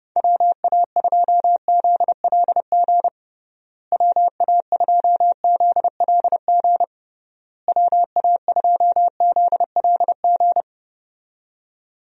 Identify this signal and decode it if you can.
Morse code, heard as a keyed tone, about 32 wpm, 700 Hz. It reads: WA2ZLG WA2ZLG WA2ZLG
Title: USA Call Signs - 6 Characters - Repeated 3x - 30wpm